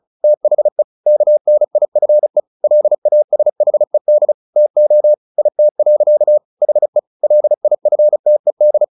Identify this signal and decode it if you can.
Morse code, heard as a keyed tone, about 34 wpm, 600 Hz. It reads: THE KNIFE LASHED TO IT. HE LIFTED
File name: the old man and the sea 2 35wpm